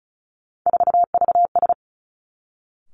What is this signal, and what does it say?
Morse code, heard as a keyed tone, about 36 wpm, 700 Hz. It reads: 4VS
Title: Three Challenging Letter-Number Combinations 35wpm